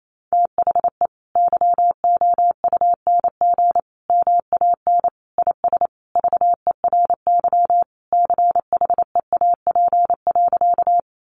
Morse code, heard as a keyed tone, about 28 wpm, 700 Hz. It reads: THE YOUNG MAN IS VERY CHEAP.
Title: Sentences from Top 1200 Words 28wpm